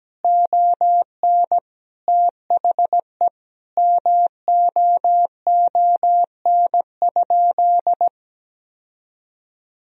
Morse code, heard as a keyed tone, about 17 wpm, 700 Hz. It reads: ON THE MOON?
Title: Sentences from Top 1000 Words - Review Part 3 - 17wpm